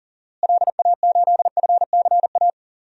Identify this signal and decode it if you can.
Morse code, heard as a keyed tone, about 40 wpm, 700 Hz. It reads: LA8FCA
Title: DXCC-100 Call Signs - 6 Characters - Repeated 2x - 40wpm